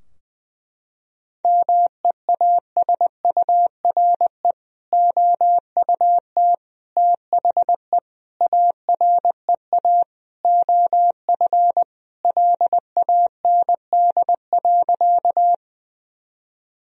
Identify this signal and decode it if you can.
Morse code, heard as a keyed tone, about 20 wpm, 700 Hz. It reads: MEASURE OUT THE AREA OF LAND.
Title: Sentences from Top 1000 Words - Review Part 2 - 20wpm